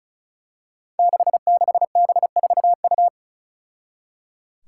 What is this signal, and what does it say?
Morse code, heard as a keyed tone, about 34 wpm, 700 Hz. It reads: 66B4U
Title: Five Challenging Letter-Number Combinations 35wpm